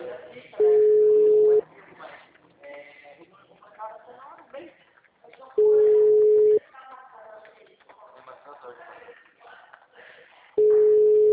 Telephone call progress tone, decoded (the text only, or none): ringback tone